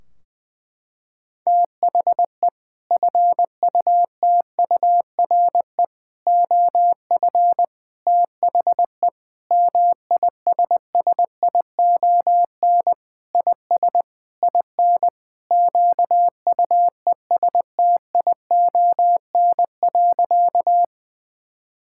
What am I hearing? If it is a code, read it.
Morse code, heard as a keyed tone, about 21 wpm, 700 Hz. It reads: THE FUTURE OF THE MISSION IS IN QUESTION.
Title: Sentences from Top 1600 Words 20wpm